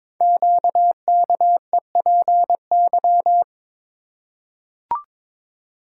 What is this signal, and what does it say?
Morse code, heard as a keyed tone, about 22 wpm, 700 Hz. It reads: QKEPY
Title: Code Groups - 5 Letters - 22wpm